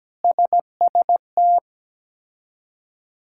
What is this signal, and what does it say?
Morse code, heard as a keyed tone, about 17 wpm, 700 Hz. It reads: SST
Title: K1USN SST QSO Practice - 17wpm